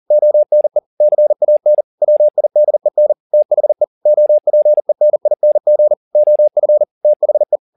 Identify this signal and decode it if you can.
Morse code, heard as a keyed tone, about 40 wpm, 600 Hz. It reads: ONE CAN WIDEN THE OPENING OF THE
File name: the old man and the sea 2 40wpm